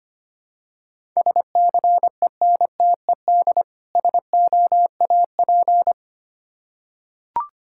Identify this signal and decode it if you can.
Morse code, heard as a keyed tone, about 25 wpm, 700 Hz. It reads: SCENTED SOAP
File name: practice / 25wpm / Difficult Words with Hints - 25wpm